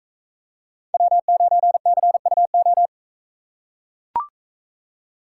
Morse code, heard as a keyed tone, about 44 wpm, 700 Hz. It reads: W9CUO